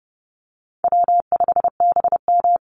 Morse code, heard as a keyed tone, about 30 wpm, 700 Hz. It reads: W5BM
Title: Easy Call Signs - 30wpm